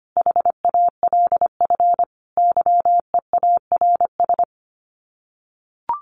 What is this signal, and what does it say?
Morse code, heard as a keyed tone, about 25 wpm, 700 Hz. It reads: HALF YEARS
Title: Sets of 5 Words Encore2 25wpm